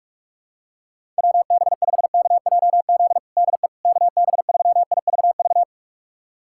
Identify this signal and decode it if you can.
Morse code, heard as a keyed tone, about 43 wpm, 700 Hz. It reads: WB5KJZ DE KB3IVV
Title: USA Callsign DE Callsign - 6 Characters - 45wpm